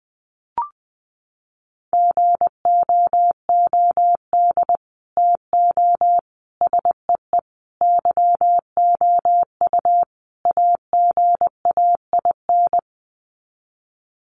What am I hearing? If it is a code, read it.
Morse code, heard as a keyed tone, about 20 wpm, 700 Hz. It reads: GOOD TO SEE YOU AGAIN
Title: Sets of 5 Words Encore2 20wpm